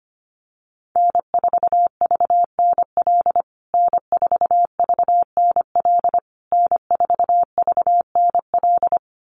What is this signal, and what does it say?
Morse code, heard as a keyed tone, about 25 wpm, 700 Hz. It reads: N4VNL N4VNL N4VNL
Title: USA Call Signs Repeated 3x 25wpm